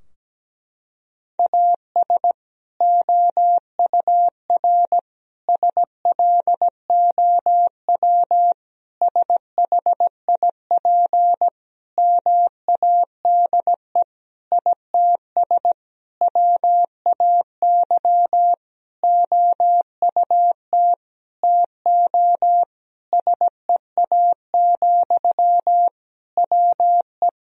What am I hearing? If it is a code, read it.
Morse code, heard as a keyed tone, about 17 wpm, 700 Hz. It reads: AS OUR SLOW SHIP MADE ITS WAY OUT TO SEA, WE
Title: Sentences from Top 1000 Words - Review Part 2 - 17wpm